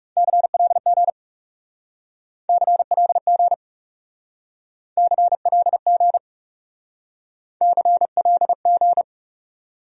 Morse code, heard as a keyed tone, about 38 wpm, 700 Hz. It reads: CLG CLG CLG CLG